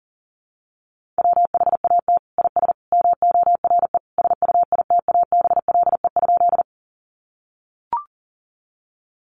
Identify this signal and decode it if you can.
Morse code, heard as a keyed tone, about 40 wpm, 700 Hz. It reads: WHAT IS MORE SUITABLE?
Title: Sentences from Top 1400 Words 40wpm